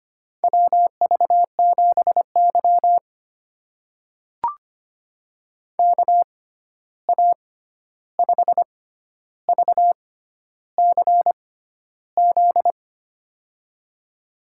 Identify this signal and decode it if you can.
Morse code, heard as a keyed tone, about 25 wpm, spaced as for 11 wpm, 700 Hz. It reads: WV7Y KA5VCZ
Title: USA Call Signs - 6x Character Spacing - 25wpm